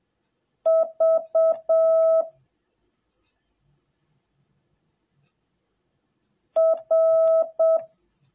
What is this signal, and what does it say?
Morse code, heard as a keyed tone, about 7 wpm, 650 Hz. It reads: V R